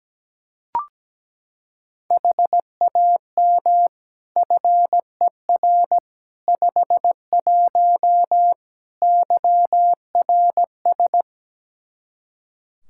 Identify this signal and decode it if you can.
Morse code, heard as a keyed tone, about 17 wpm, 700 Hz. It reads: HAM FER 51 YRS
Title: Practice QSO - Exchange 3 - More Info - 17wpm